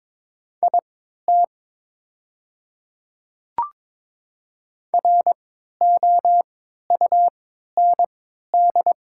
Morse code, heard as a keyed tone, about 22 wpm, spaced as for 12 wpm, 700 Hz. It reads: IT ROUND
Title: 3x Character Spacing - Top 200 Words - 22wpm